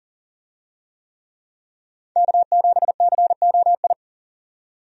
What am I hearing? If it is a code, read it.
Morse code, heard as a keyed tone, about 40 wpm, 700 Hz. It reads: K7COI
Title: USA Call Signs - 5 Characters - 40wpm